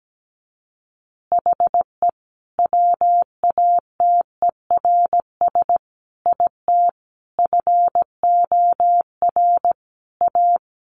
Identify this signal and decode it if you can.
Morse code, heard as a keyed tone, about 17 wpm, 700 Hz. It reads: HE WATERS IT FOR A
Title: Easy Phrases 17wpm